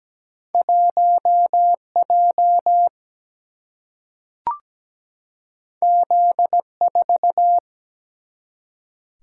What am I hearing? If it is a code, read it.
Morse code, heard as a keyed tone, about 17 wpm, 700 Hz. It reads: 1J Z4
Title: Single-Letter Single-Number 17wpm